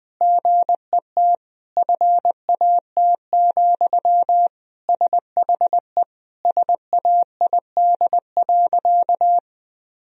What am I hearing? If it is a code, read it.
Morse code, heard as a keyed tone, about 20 wpm, 700 Hz. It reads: GET FAT, SHE SAID.
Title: Sentences from Top 1000 Words - Review Part 4 - 20wpm